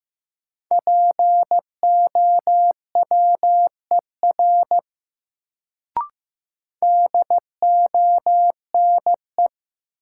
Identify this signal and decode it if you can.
Morse code, heard as a keyed tone, about 15 wpm, 700 Hz. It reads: POWER DONE
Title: Sentences from Top 500 Words - New Word Drill - 15wpm